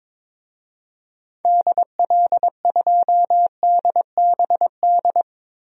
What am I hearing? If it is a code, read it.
Morse code, heard as a keyed tone, about 21 wpm, 700 Hz. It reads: DL2DBD
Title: DXCC-100 Call Signs - 6 Characters - Repeated 2x - 22wpm